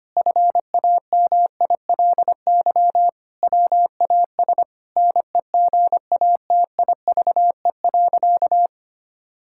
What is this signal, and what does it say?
Morse code, heard as a keyed tone, about 25 wpm, 700 Hz. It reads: FAMILY WAS NEGATIVE.
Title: Sentences from Top 1000 Words - Review Part 6 - 25wpm